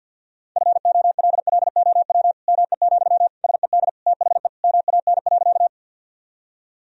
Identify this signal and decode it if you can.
Morse code, heard as a keyed tone, about 52 wpm, 700 Hz. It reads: FOLLOW ME, SED THE MAN.
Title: Sentences from Top 200 Words and Abbreviations 50wpm